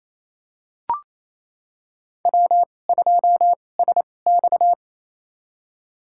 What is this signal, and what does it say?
Morse code, heard as a keyed tone, about 28 wpm, spaced as for 20 wpm, 700 Hz. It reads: W2SX
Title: USA Call Signs - 4 Characters - 2x Character Spacing - 28wpm